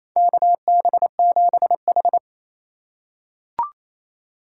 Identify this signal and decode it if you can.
Morse code, heard as a keyed tone, about 28 wpm, 700 Hz. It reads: KB7H
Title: K1USN SST QSO Practice - 28wpm